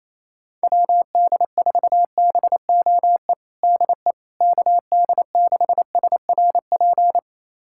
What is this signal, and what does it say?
Morse code, heard as a keyed tone, about 28 wpm, 700 Hz. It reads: WD4BOE DE KD6SRP
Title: USA Callsign DE Callsign 28wpm